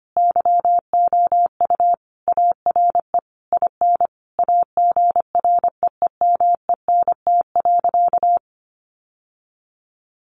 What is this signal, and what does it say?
Morse code, heard as a keyed tone, about 25 wpm, 700 Hz. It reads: YOU ARE IN AGREEMENT.
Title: Sentences from Top 1500 Words 25wpm